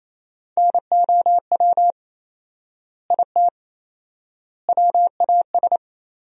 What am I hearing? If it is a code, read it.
Morse code, heard as a keyed tone, about 28 wpm, 700 Hz. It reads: NOW IT WAS
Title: 4x Word-Spacing - Sentences from Top 200 Words - 28wpm